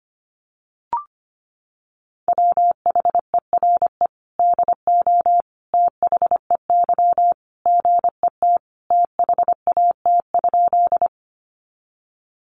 Morse code, heard as a keyed tone, about 25 wpm, 700 Hz. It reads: WHERE DO THEY GET THAT?
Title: Sentences from Top 200 Words 25wpm